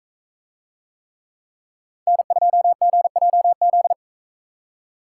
Morse code, heard as a keyed tone, about 42 wpm, 700 Hz. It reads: N2GJZ